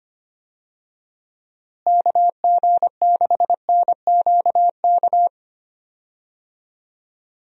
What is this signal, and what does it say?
Morse code, heard as a keyed tone, about 25 wpm, 700 Hz. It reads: KG6NQK
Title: USA Call Signs - 6 Characters - Repeated 2x - 25wpm